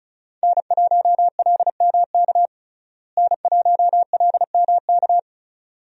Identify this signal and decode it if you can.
Morse code, heard as a keyed tone, about 36 wpm, 700 Hz. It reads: N1LMK N1LMK
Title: USA Call Signs - 5 Characters - Repeated 2x - 35wpm